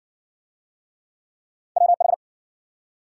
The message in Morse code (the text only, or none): WH